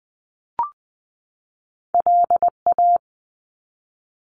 Morse code, heard as a keyed tone, about 20 wpm, 700 Hz. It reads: LA
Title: US States 20wpm